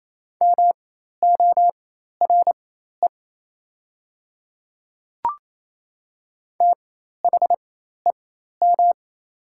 Morse code, heard as a keyed tone, about 28 wpm, spaced as for 13 wpm, 700 Hz. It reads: MORE THEM
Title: 4x Character Spacing - Top 500 Words - 28wpm